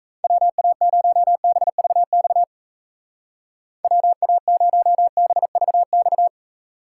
Morse code, heard as a keyed tone, about 38 wpm, 700 Hz. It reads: WA0BVX WA0BVX